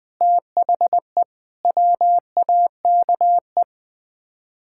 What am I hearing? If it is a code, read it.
Morse code, heard as a keyed tone, about 20 wpm, 700 Hz. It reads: THE WAKE
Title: Build A Phrase - 4-Word Phrases - 20wpm